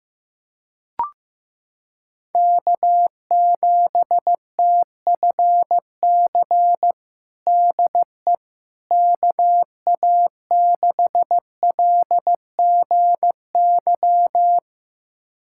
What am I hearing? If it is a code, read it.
Morse code, heard as a keyed tone, about 15 wpm, 700 Hz. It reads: K7TFC DE KA6LGY
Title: USA Callsign DE Callsign 15wpm